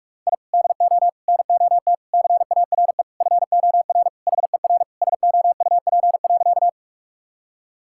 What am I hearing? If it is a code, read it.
Morse code, heard as a keyed tone, about 46 wpm, 700 Hz. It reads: I DO NOT CARE FOR HER SOUP.
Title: Sentences from Top 1000 Words - Review Part 6 - 45wpm